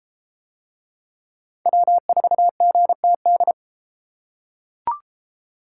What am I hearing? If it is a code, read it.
Morse code, heard as a keyed tone, about 30 wpm, 700 Hz. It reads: W4GTD